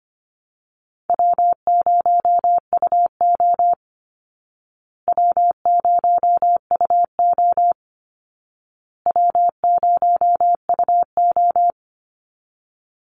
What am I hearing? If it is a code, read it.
Morse code, heard as a keyed tone, about 25 wpm, 700 Hz. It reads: W0UO W0UO W0UO
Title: Call Signs Repeated 2x - 25wpm